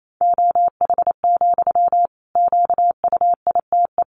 Morse code, heard as a keyed tone, about 28 wpm, 700 Hz. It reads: OH, QUITE
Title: Sentences from Top 700 Words 28wpm